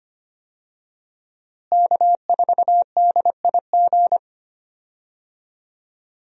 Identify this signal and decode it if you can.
Morse code, heard as a keyed tone, about 25 wpm, 700 Hz. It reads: K4DIG
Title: USA Call Signs - 5 Characters - Repeated 2x - 25wpm